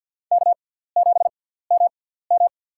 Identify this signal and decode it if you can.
Morse code, heard as a keyed tone, about 50 wpm, spaced as for 17 wpm, 700 Hz. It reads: K7MM
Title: USA Call Signs - 4 Characters - 6x Character Spacing - 50wpm